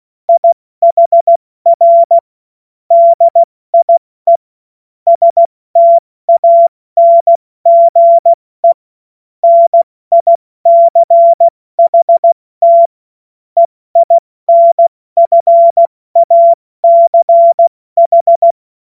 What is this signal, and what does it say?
Morse code, heard as a keyed tone, about 16 wpm, spaced as for 14 wpm, 680 Hz. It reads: IHR DIE STANGE NICHT EINFACH
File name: Morsekurs / Witze / Witz 10.Oktober 2023 70 F80